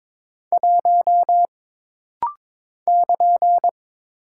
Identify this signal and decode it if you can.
Morse code, heard as a keyed tone, about 22 wpm, 700 Hz. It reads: 1 <KN>
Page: This rapid-fire timing is geared to push your ICR (Instant Character Recognition) skills.